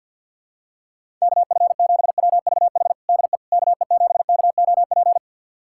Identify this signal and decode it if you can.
Morse code, heard as a keyed tone, about 48 wpm, 700 Hz. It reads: KF7WVH DE KE7KOP
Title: USA Callsign DE Callsign - 6 Characters - 50wpm